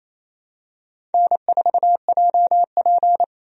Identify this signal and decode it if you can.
Morse code, heard as a keyed tone, about 27 wpm, 700 Hz. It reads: N4JP
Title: USA Call Signs - 4 Characters - 28wpm